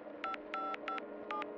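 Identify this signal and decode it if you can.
DTMF, heard as telephone touch tones, tones of about 220 ms. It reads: 5557
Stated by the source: DTMF from a telephone dial-up audiobook.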